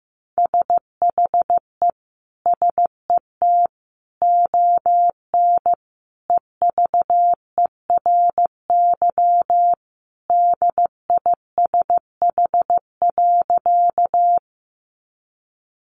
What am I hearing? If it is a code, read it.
Morse code, heard as a keyed tone, about 15 wpm, 700 Hz. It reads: SHE SET ON EVERY DISH.
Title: Sentences from Top 1300 Words 15wpm